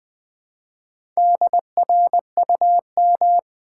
Morse code, heard as a keyed tone, about 20 wpm, 700 Hz. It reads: DRUM
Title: Build A Word - 1 Letter At A Time - 4-Letter Words - 20wpm